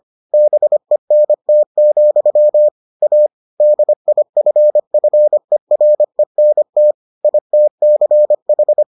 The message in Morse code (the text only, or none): BENT, A DIFFERENT ITCH